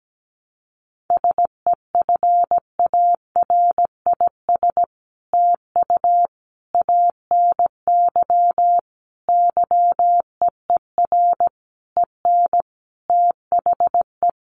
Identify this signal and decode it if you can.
Morse code, heard as a keyed tone, about 17 wpm, 700 Hz. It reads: SEFARIS TU ANY YEER EN THE
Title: A Sound Of Thunder 17wpm